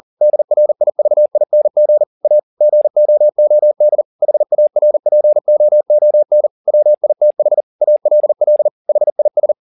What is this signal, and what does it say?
Morse code, heard as a keyed tone, about 40 wpm, 600 Hz. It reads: DRIVING A GOOD HARPOON WITH ALL HIS
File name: the old man and the sea 2 40wpm